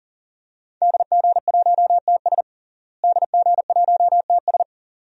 Morse code, heard as a keyed tone, about 40 wpm, 700 Hz. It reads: DG1TS DG1TS
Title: DXCC-100 Call Signs - 5 Characters - Repeated 2x - 40wpm